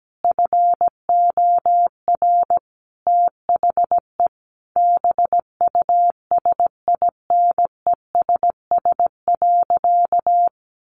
Morse code, heard as a keyed tone, about 17 wpm, 700 Hz. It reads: FOR THE BUSINESS.
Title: Sentences from Top 1400 Words 17wpm